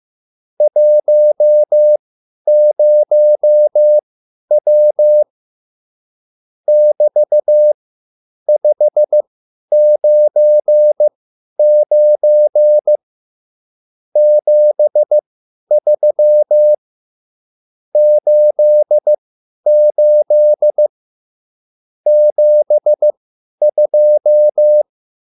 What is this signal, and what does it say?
Morse code, heard as a keyed tone, about 15 wpm, spaced as for 10 wpm, 600 Hz. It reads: 10W <BT>599 73 88 72
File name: QSL / CW lesson